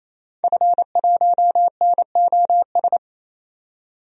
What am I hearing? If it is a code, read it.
Morse code, heard as a keyed tone, about 27 wpm, 700 Hz. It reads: F1NOS
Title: DXCC-100 Call Signs - 5 Characters - 28wpm